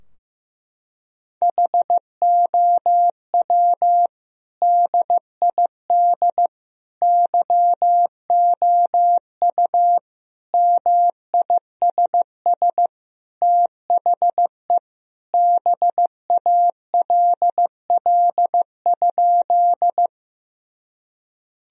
Morse code, heard as a keyed tone, about 15 wpm, 700 Hz. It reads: HOW DID YOU MISS THE BALL?
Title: Sentences from Top 1000 Words - Review Part 3 - 15wpm